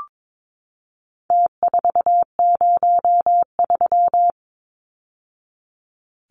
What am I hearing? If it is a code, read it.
Morse code, heard as a keyed tone, about 22 wpm, 700 Hz. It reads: T403
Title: Numbers - Contest Style - 4-Digit 22wpm